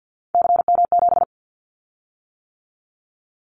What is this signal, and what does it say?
Morse code, heard as a keyed tone, about 49 wpm, 700 Hz. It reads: CM7
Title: Callsign Prefixes for Top 100 DXCC - 50wpm